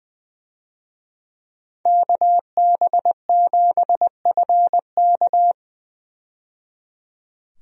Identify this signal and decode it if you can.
Morse code, heard as a keyed tone, about 20 wpm, 700 Hz. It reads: KB7FK
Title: USA Call Signs - 5 Characters - 20wpm